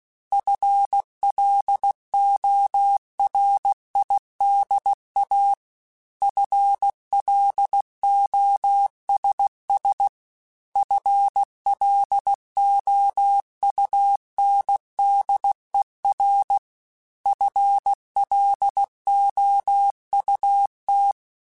Morse code, heard as a keyed tone, about 16 wpm, 800 Hz. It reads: FLORIDA FLOSS FLOUNDER FLOUT
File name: FL Runs / FL Runs mp3s / FL Runs @15